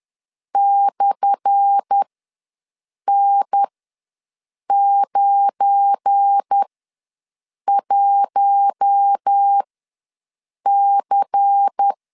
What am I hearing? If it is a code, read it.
Morse code, heard as a keyed tone, about 10 wpm, spaced as for 6 wpm, 800 Hz. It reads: /N91C
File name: LEZIONI / 10 SIGNS / 10D COMA ws 10wpm 4314